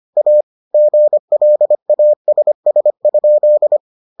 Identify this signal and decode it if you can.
Morse code, heard as a keyed tone, about 25 wpm, 600 Hz. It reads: A GLASS?